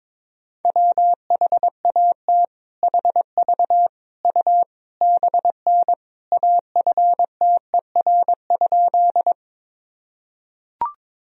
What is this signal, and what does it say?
Morse code, heard as a keyed tone, about 22 wpm, 700 Hz. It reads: WHAT HV U BN AFTER?
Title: Sentences from Top 200 Words and Abbreviations 22wpm